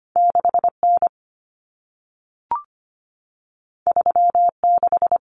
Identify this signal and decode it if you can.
Morse code, heard as a keyed tone, about 25 wpm, 700 Hz. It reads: 6N 36
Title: Common Cut Numbers - 2-Digit 25wpm